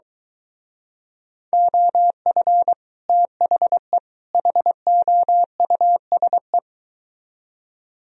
Morse code, heard as a keyed tone, about 23 wpm, 700 Hz. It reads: OF THE HOUSE